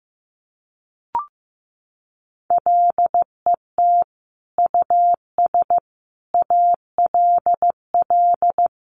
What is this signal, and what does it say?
Morse code, heard as a keyed tone, about 15 wpm, 700 Hz. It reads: LET US ALL
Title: Sentences from Top 1200 Words 15wpm